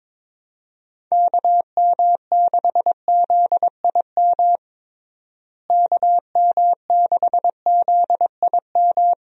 Morse code, heard as a keyed tone, about 22 wpm, 700 Hz. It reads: KM6ZIM KM6ZIM
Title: USA Call Signs - 6 Characters - Repeated 2x - 22wpm